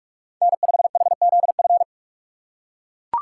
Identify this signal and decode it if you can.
Morse code, heard as a keyed tone, about 40 wpm, 700 Hz. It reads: N5HZF